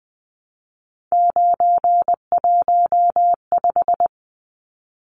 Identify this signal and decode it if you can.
Morse code, heard as a keyed tone, about 20 wpm, 700 Hz. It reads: 915
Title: Numbers - 3-Digit 20wpm